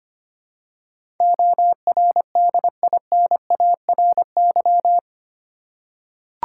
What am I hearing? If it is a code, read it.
Morse code, heard as a keyed tone, about 25 wpm, 700 Hz. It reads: ORDINARY
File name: 8 Letter Words - First 3 Letters Given - 25wpm